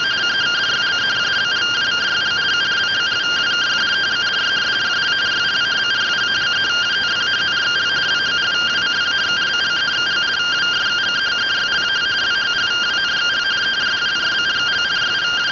Two-tone teleprinter signal, about 170 Hz shift.